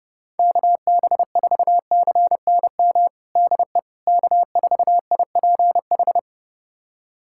Morse code, heard as a keyed tone, about 29 wpm, 700 Hz. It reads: KB4CNM DE K4IPH
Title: USA Callsign DE Callsign 30wpm